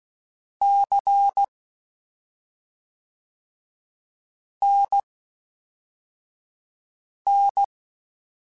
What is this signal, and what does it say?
Morse code, heard as a keyed tone, about 16 wpm, spaced as for 3 wpm, 800 Hz. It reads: CNN
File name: CW Comparisons / CW Comparisons-mp3s / CW Comparisons Broken Characters @5